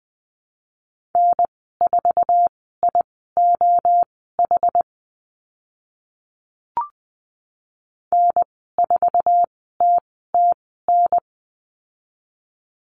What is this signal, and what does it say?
Morse code, heard as a keyed tone, about 20 wpm, spaced as for 14 wpm, 700 Hz. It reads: N4IOH N4TTN
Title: 082 Lesson 14g - Number 2 - Call Signs 20wpm